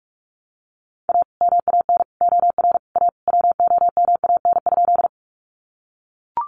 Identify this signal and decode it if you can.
Morse code, heard as a keyed tone, about 44 wpm, 700 Hz. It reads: A MAN OR A WOMAN?
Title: Sentences from Top 1100 Words 45wpm